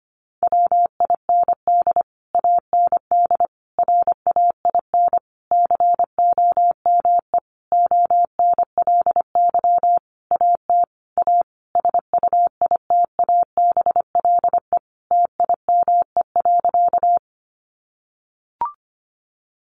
Morse code, heard as a keyed tone, about 25 wpm, 700 Hz. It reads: WIND AND RAIN COME ONLY AT A SUITABLE TIME.
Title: Sentences from Top 1400 Words 25wpm